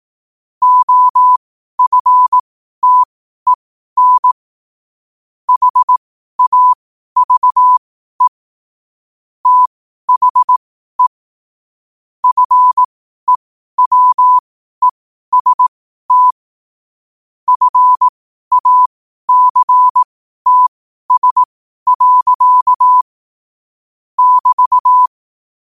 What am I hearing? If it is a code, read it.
Morse code, heard as a keyed tone, about 18 wpm, spaced as for 12 wpm, 1000 Hz. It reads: OFTEN HAVE THE FEWEST FACTS. <BT>
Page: Quotes for Wed, 13 Aug 2025 in Morse Code at 12 words per minute.